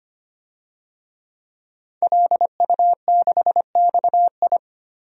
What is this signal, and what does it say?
Morse code, heard as a keyed tone, about 25 wpm, 700 Hz. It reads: LU6XI
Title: DXCC-100 Call Signs - 5 Characters - 25wpm